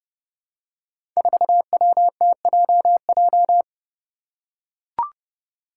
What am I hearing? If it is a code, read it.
Morse code, heard as a keyed tone, about 30 wpm, 700 Hz. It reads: 4WTJJ
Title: Five Challenging Letter-Number Combinations 30wpm